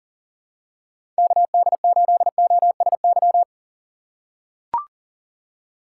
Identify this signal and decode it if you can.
Morse code, heard as a keyed tone, about 40 wpm, 700 Hz. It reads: KD8OSY
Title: USA Call Signs - 6x Character Spacing - 40wpm